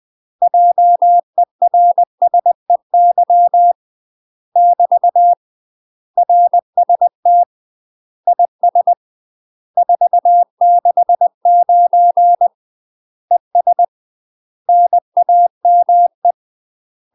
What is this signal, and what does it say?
Morse code, heard as a keyed tone, about 20 wpm, 700 Hz. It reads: JERSEY <BT> RST IS 469 ES NAME
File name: UK MASTER QSOs TRAINING / 20WPM / UK MASTER QSOs 0066